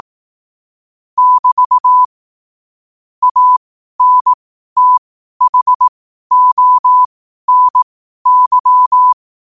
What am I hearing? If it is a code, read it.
Morse code, heard as a keyed tone, about 18 wpm, spaced as for 12 wpm, 1000 Hz. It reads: <BT> ANTHONY
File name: qotdfiles / 12wpm